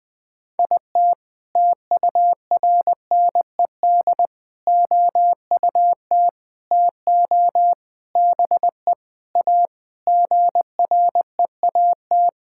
Morse code, heard as a keyed tone, about 20 wpm, 700 Hz. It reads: IT TURNED OUT TO BE A GREAT
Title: Sentences with 5-Word Phrases - 20wpm